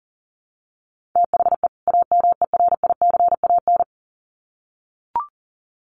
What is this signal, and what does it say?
Morse code, heard as a keyed tone, about 40 wpm, 700 Hz. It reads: THE AMERICAN
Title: Sets of 2 Words 40wpm